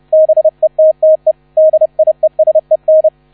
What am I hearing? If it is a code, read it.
Morse code, heard as a keyed tone, about 31 wpm, 630 Hz, with radio noise. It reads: BETTE DIESEN